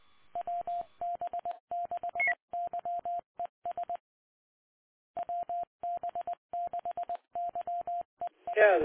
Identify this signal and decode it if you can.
Morse code, heard as a keyed tone, about 20 wpm, 700 Hz, with speech. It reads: WB6YES WB6YES